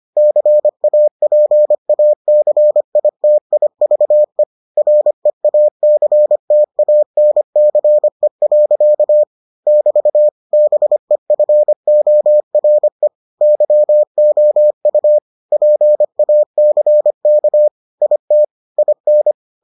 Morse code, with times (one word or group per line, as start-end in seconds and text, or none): 0.2-4.4 CAPACITIVE
4.8-9.2 REACTANCE.
9.7-10.3 <BT>
10.5-13.1 BEFORE
13.4-15.2 YOU
15.5-17.7 PACK
18.0-18.4 IT
18.8-19.3 IN